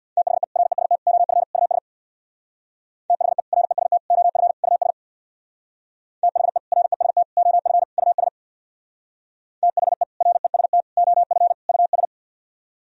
Morse code, heard as a keyed tone, about 64 wpm, 700 Hz. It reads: THE REST OF US THE REST OF US THE REST OF US THE REST OF US